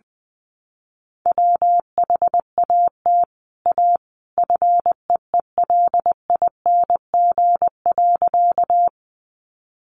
Morse code, heard as a keyed tone, about 20 wpm, 700 Hz. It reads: WHAT A FEELING.
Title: Sentences from Top 1200 Words 20wpm